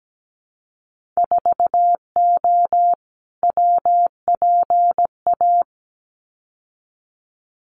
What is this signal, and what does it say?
Morse code, heard as a keyed tone, about 17 wpm, 700 Hz. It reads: 4O WPA
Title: Winter Field Day Exchanges 17wpm